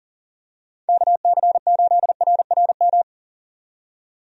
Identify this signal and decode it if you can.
Morse code, heard as a keyed tone, about 40 wpm, 700 Hz. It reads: KC8RRM
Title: USA Call Signs - 6 Characters - 40wpm